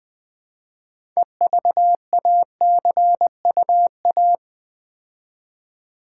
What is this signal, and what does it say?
Morse code, heard as a keyed tone, about 20 wpm, 700 Hz. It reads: EVACUA
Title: Build A Word - 3 Letters At A Time - 8-Letter Words - 20wpm